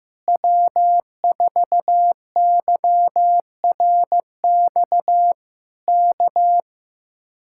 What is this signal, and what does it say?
Morse code, heard as a keyed tone, about 15 wpm, 700 Hz. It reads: W4YRX K
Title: Speed-Racing Common QSO Phrases Encore - 15wpm